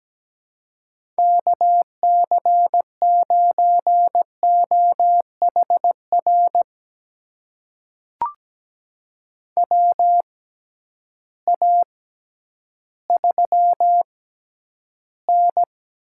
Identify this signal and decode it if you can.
Morse code, heard as a keyed tone, about 17 wpm, spaced as for 10 wpm, 700 Hz. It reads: KC9OHR WA3N
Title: USA Call Signs - 6 Characters - 6x Character Spacing - 17wpm